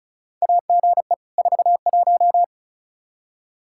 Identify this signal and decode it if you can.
Morse code, heard as a keyed tone, about 34 wpm, 700 Hz. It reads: AGE 41
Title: Practice QSO - Exchange 3 - More Info - 35wpm